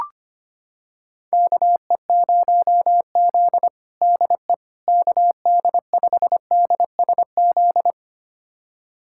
Morse code, heard as a keyed tone, about 25 wpm, 700 Hz. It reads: KE0Z DE KD5DSZ
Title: USA Callsign DE Callsign 25wpm